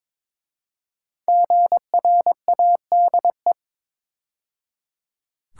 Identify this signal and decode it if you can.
Morse code, heard as a keyed tone, about 22 wpm, 700 Hz. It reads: GRADE
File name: 5 Letter Words - First 3 Letters Given - 22wpm